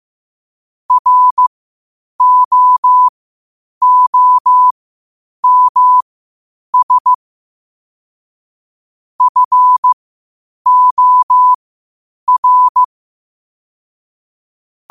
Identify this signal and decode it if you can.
Morse code, heard as a keyed tone, about 15 wpm, spaced as for 8 wpm, 1000 Hz. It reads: ROOMS FOR
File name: qotdfiles / 8wpm